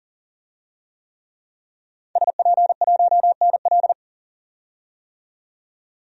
Morse code, heard as a keyed tone, about 40 wpm, 700 Hz. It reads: SP1NL
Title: DXCC-100 Call Signs - 5 Characters - 40wpm